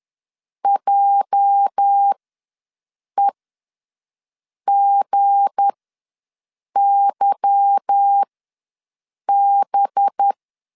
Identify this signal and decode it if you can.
Morse code, heard as a keyed tone, about 10 wpm, spaced as for 6 wpm, 800 Hz. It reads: JEGYB
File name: LEZIONI / 8 YJ45 / 8 YJ ws 10wpm 4043